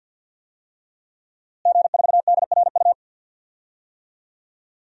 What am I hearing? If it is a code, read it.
Morse code, heard as a keyed tone, about 51 wpm, 700 Hz. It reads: G4DRU
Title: DXCC-100 Call Signs - 5 Characters - Repeated 3x - 50wpm